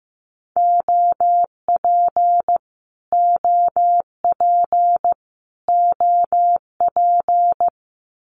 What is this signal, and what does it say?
Morse code, heard as a keyed tone, about 15 wpm, 700 Hz. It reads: OP OP OP
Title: Abbreviations Repeated 3x - 15wpm